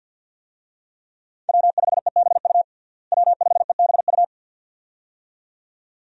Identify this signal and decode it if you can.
Morse code, heard as a keyed tone, about 48 wpm, 700 Hz. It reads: W5EBU W5EBU
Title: USA Call Signs - 5 Characters - Repeated 2x - 50wpm